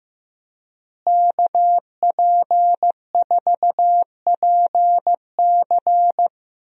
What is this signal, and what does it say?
Morse code, heard as a keyed tone, about 15 wpm, 700 Hz. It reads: KP4PC
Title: DXCC-100 Call Signs - 5 Characters - Repeated 3x - 15wpm